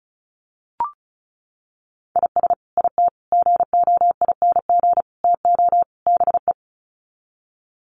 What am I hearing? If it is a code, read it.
Morse code, heard as a keyed tone, about 34 wpm, 700 Hz. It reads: IS IT GOING TO BE
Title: Sets of 5 Words Encore2 35wpm